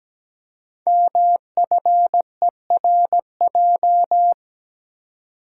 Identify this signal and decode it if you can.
Morse code, heard as a keyed tone, about 17 wpm, 700 Hz. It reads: MFERJ
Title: Code Groups - 5 Letters - 17wpm